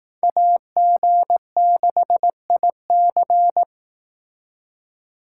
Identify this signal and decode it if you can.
Morse code, heard as a keyed tone, about 18 wpm, 700 Hz. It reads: AG6IC
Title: Speed Racing - USA 2x2 Call Signs 15wpm